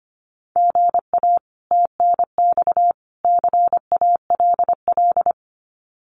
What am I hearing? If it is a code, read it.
Morse code, heard as a keyed tone, about 25 wpm, 700 Hz. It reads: GA TNX CALL
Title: Common QSO Phrases 25wpm